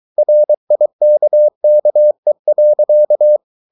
Morse code, heard as a keyed tone, about 23 wpm, 600 Hz. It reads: RIKKE.
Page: Ferie 23 wpm